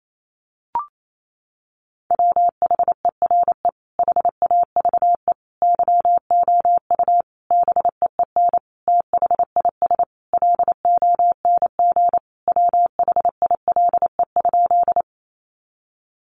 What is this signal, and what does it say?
Morse code, heard as a keyed tone, about 28 wpm, 700 Hz. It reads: WHERE HAVE YOU BEEN THIS LONG WHILE?
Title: Sentences from Top 300 Words 28wpm